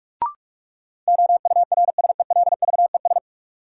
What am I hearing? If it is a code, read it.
Morse code, heard as a keyed tone, about 46 wpm, 700 Hz. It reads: OURSELVES
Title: Sentences from Top 1600 Words - New Word Drill - 45wpm